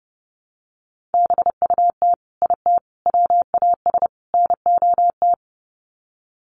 Morse code, heard as a keyed tone, about 28 wpm, 700 Hz. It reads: BUT IT WAS NOT
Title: Sets of 4 Words 30wpm